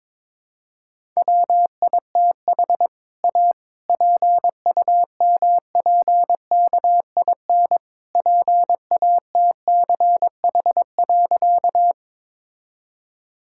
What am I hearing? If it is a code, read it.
Morse code, heard as a keyed tone, about 22 wpm, 700 Hz. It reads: WITH A PUMPKIN PATCH.
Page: morse-code-ninja-practice-22wpm